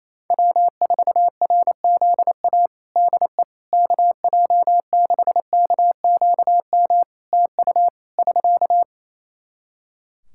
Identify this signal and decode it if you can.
Morse code, heard as a keyed tone, about 28 wpm, 700 Hz. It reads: W4RZA DE KJ6KQM TU <SK>
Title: Practice QSO - Exchange 4 - Ending - 28wpm